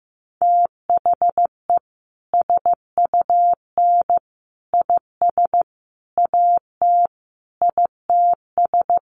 Morse code, heard as a keyed tone, about 15 wpm, 700 Hz. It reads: THE SUN IS AT ITS
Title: Sentences from Top 1300 Words 15wpm